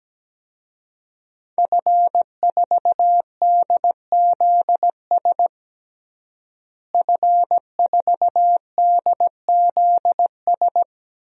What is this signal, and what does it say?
Morse code, heard as a keyed tone, about 17 wpm, 700 Hz. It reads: F4DZS F4DZS
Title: DXCC-100 Call Signs - 5 Characters - Repeated 3x - 17wpm